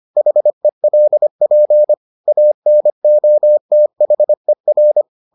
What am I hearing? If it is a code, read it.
Morse code, heard as a keyed tone, about 25 wpm, 600 Hz. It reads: HELP ANOTHER